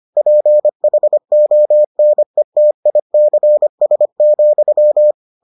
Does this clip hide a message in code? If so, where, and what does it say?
Yes: Morse code from 0.2 to 5.1 s, heard as a keyed tone, about 25 wpm, 600 Hz. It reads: PHONETICS,